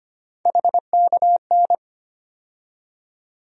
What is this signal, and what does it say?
Morse code, heard as a keyed tone, about 25 wpm, 700 Hz. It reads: HKN
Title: USA Call Sign Postfixes - 3 Letters 25wpm